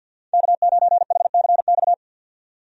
Morse code, heard as a keyed tone, about 46 wpm, 700 Hz. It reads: K9HCX
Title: USA Call Signs - 5 Characters - 50wpm